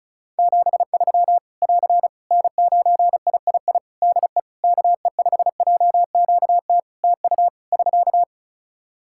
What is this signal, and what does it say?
Morse code, heard as a keyed tone, about 36 wpm, 700 Hz. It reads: 73 <AR> N9III DE KE5JQT TU <SK>
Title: Practice QSO - Exchange 4 - Ending - 35wpm